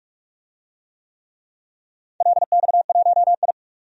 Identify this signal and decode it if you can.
Morse code, heard as a keyed tone, about 47 wpm, 700 Hz. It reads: LX1I
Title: DXCC-100 Call Signs - 4 Characters - 45wpm